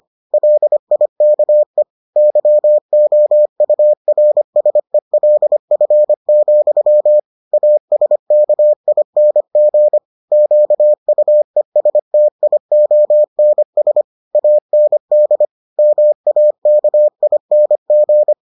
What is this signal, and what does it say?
Morse code, heard as a keyed tone, about 25 wpm, 600 Hz. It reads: LIKE YOURSELF, ASKING QUESTIONS AND MAKING